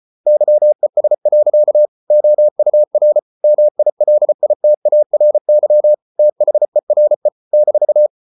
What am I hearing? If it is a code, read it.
Morse code, heard as a keyed tone, about 34 wpm, 600 Hz. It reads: YES. OUR MILITARY THERE -